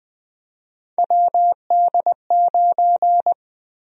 Morse code, heard as a keyed tone, about 20 wpm, 700 Hz. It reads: WD9
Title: USA Call Sign Prefixes - Repeated 2x - 20wpm